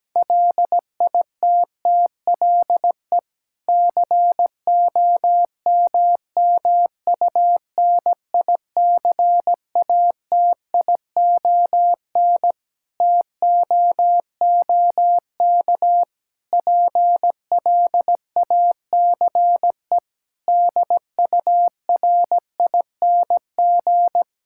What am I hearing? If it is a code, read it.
Morse code, heard as a keyed tone, about 17 wpm, 700 Hz. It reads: LITTLE COMMUNICATION TOOK PLACE DURING
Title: Sentences from Top 1000 Words - Review Part 6 - 17wpm